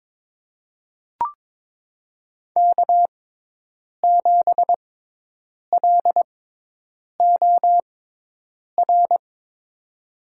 Morse code, heard as a keyed tone, about 22 wpm, spaced as for 8 wpm, 700 Hz. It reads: K7LOR
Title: USA Call Signs - 5 Characters - 6x Character Spacing - 22wpm